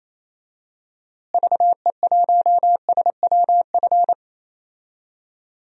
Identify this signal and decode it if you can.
Morse code, heard as a keyed tone, about 28 wpm, 700 Hz. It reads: VE1SWF
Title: DXCC-100 Call Signs - 6 Characters - Repeated 2x - 28wpm